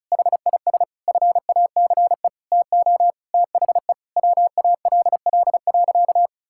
Morse code, heard as a keyed tone, about 36 wpm, 700 Hz. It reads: HIS FACE TO THE WALL.
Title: Sentences from Top 1000 Words - Review Part 6 - 35wpm